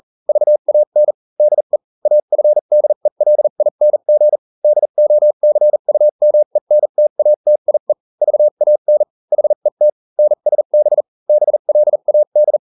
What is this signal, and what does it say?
Morse code, heard as a keyed tone, about 40 wpm, spaced as for 35 wpm, 600 Hz. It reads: VAN DE AFDELING DOCUMENTATIE VAN HET NSB BLAD